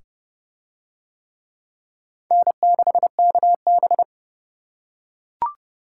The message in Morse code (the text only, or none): N6KB